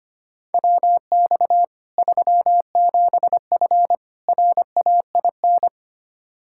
Morse code, heard as a keyed tone, about 25 wpm, 700 Hz. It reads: WX 37F RAIN
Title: Practice QSO - Exchange 2 - Info - 25wpm